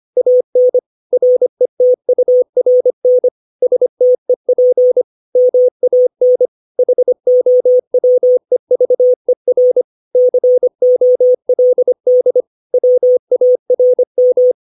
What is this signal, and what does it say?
Morse code, heard as a keyed tone, about 25 wpm, 500 Hz. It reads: AN RETURN STEP MAN HOWEVER COLD WARM
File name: 632 words / 632 Words 25WPM